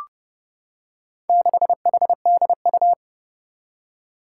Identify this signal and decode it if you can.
Morse code, heard as a keyed tone, about 29 wpm, 700 Hz. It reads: 6HDU
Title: Four Challenging Letter-Number Combinations 30wpm